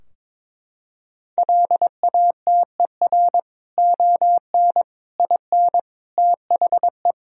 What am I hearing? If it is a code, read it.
Morse code, heard as a keyed tone, about 22 wpm, 700 Hz. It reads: LATER ON IN THE
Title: Sentences from Top 1000 Words - Review Part 4 - 22wpm